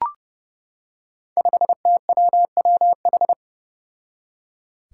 Morse code, heard as a keyed tone, about 30 wpm, 700 Hz. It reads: 5TWWH
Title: Five Challenging Letter-Number Combinations 30wpm